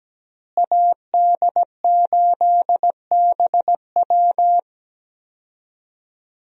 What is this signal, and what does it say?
Morse code, heard as a keyed tone, about 17 wpm, 700 Hz. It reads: AD8BW
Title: USA Call Signs - 5 Characters - Repeated 2x - 17wpm